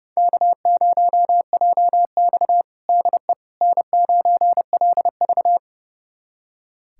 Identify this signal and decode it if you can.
Morse code, heard as a keyed tone, about 32 wpm, 700 Hz. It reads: K0JX DE N9LV
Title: USA Callsign DE Callsign - 4 Characters - 30wpm